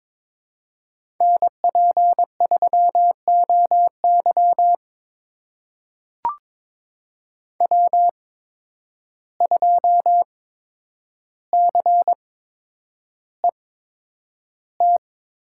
Morse code, heard as a keyed tone, about 23 wpm, spaced as for 10 wpm, 700 Hz. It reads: NP3OY W2CET
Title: USA Call Signs - 8x Character Spacing - 22wpm